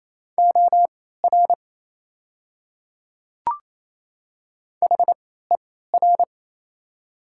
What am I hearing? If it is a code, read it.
Morse code, heard as a keyed tone, about 28 wpm, spaced as for 16 wpm, 700 Hz. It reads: OR HER
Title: 3x Character Spacing - Top 100 Words - 28wpm